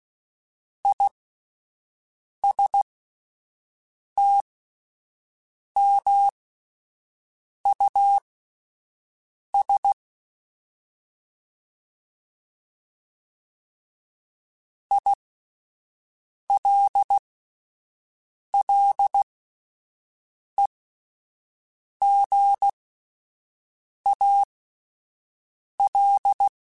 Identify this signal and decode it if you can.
Morse code, heard as a keyed tone, about 16 wpm, spaced as for 5 wpm, 800 Hz. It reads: ISTMUS ILLEGAL